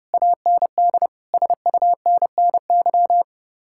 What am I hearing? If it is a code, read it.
Morse code, heard as a keyed tone, about 39 wpm, 700 Hz. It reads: AND SUNNY